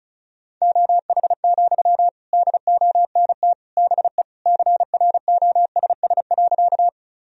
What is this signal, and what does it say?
Morse code, heard as a keyed tone, about 36 wpm, 690 Hz. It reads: OH, DONT BE CROSS.
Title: Sentences from Top 1000 Words - Review Part 1 - 35wpm